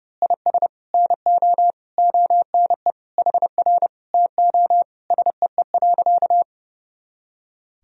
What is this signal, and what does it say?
Morse code, heard as a keyed tone, about 30 wpm, 700 Hz. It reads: IS NO ONE HR TO SEE.
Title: Sentences from Top 200 Words and Abbreviations 30wpm